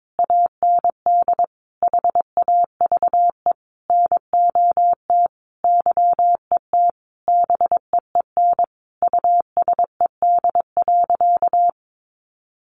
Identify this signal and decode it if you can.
Morse code, heard as a keyed tone, about 22 wpm, 700 Hz. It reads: AND HAVE NOT YET BEEN USED.
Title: Sentences from Top 700 Words 22wpm